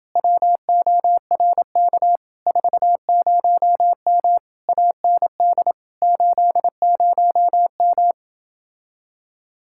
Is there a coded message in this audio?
Yes: Morse code from 0.2 to 8.1 s, heard as a keyed tone, about 28 wpm, 700 Hz. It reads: WORK 40M AND 80M